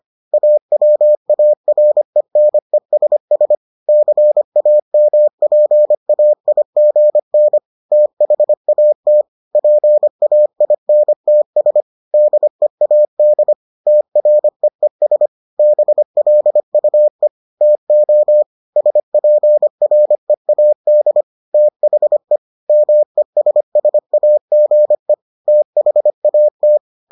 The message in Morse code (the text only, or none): AWARENESS CAMPAIGN THAT PAINTS DEAD TREES BLUE TO SPREAD THE MESSAGE THAT